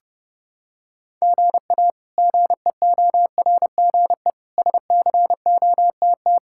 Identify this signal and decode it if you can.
Morse code, heard as a keyed tone, about 29 wpm, 700 Hz. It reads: GA GEORGE SCOTT
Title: K1USN SST QSO Practice - 30wpm